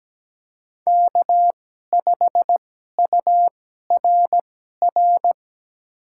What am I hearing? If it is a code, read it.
Morse code, heard as a keyed tone, about 17 wpm, spaced as for 12 wpm, 700 Hz. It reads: K5URR
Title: USA Call Signs - 5 Characters - 2x Character Spacing - 17wpm